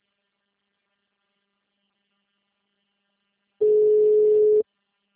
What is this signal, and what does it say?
Ringback tone (Europe, 425 Hz).